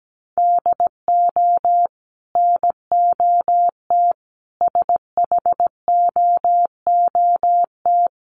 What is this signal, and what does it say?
Morse code, heard as a keyed tone, about 17 wpm, 700 Hz. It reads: DO NOT SHOOT
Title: Sentences from Top 1200 Words 17wpm